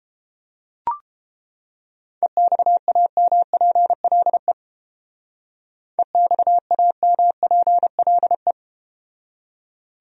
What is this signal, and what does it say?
Morse code, heard as a keyed tone, about 32 wpm, 700 Hz. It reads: EXAMPLE EXAMPLE